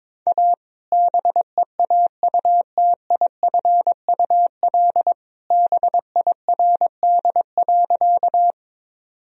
Morse code, heard as a keyed tone, about 22 wpm, 700 Hz. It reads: A BEAUTIFUL BIRD.
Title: Sentences from Top 1000 Words - Review Part 2 - 22wpm